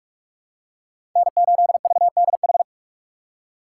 Morse code, heard as a keyed tone, about 46 wpm, 700 Hz. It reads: N8VDH
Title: USA Call Signs - 5 Characters - 3x Character Spacing - 45wpm